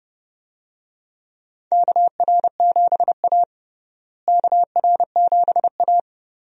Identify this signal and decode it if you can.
Morse code, heard as a keyed tone, about 30 wpm, 700 Hz. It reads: KR7A KR7A
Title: USA Call Signs - 4 Characters - Repeated 2x - 30wpm